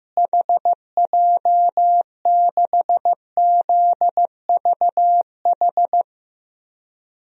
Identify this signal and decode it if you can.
Morse code, heard as a keyed tone, about 15 wpm, 700 Hz. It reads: HJ6ZVH
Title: DXCC-100 Call Signs - 6 Characters - Repeated 3x - 15wpm